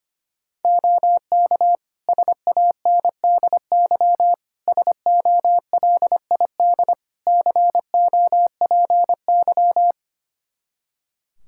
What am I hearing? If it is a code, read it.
Morse code, heard as a keyed tone, about 25 wpm, 700 Hz. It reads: OK SANDY SOLID COPY
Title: Practice QSO - Exchange 3 - More Info - 25wpm